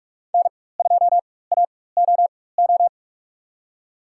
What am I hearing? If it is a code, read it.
Morse code, heard as a keyed tone, about 45 wpm, spaced as for 21 wpm, 700 Hz. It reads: N2AOO
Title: USA Call Signs - 5 Characters - 4x Character Spacing - 45wpm